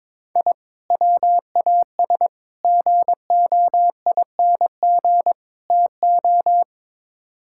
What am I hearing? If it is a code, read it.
Morse code, heard as a keyed tone, about 22 wpm, 700 Hz. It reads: I WAS GOING TO
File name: practice / 22wpm / Build A Phrase - 5-Word Phrases - 22wpm